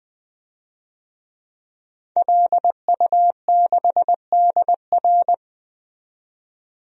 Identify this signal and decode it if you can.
Morse code, heard as a keyed tone, about 20 wpm, 700 Hz. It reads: LU6DR
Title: DXCC-100 Call Signs - 5 Characters - 20wpm